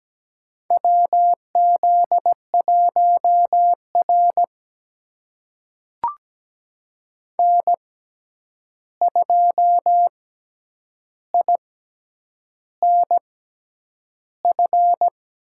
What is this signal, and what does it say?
Morse code, heard as a keyed tone, about 17 wpm, spaced as for 8 wpm, 700 Hz. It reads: WZ1R N2INF
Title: USA Call Signs - 6x Character Spacing - 17wpm